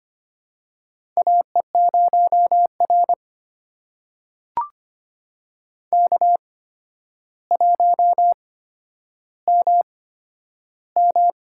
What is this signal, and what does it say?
Morse code, heard as a keyed tone, about 25 wpm, spaced as for 11 wpm, 700 Hz. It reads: AE0R K1MM
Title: USA Call Signs - 4 Characters - 8x Character Spacing - 25wpm